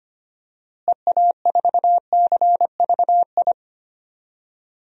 Morse code, heard as a keyed tone, about 25 wpm, 700 Hz. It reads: EA4CVI
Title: DXCC-100 Call Signs - 6 Characters - Repeated 2x - 25wpm